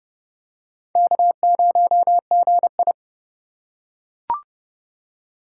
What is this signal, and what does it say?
Morse code, heard as a keyed tone, about 28 wpm, 700 Hz. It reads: K0GI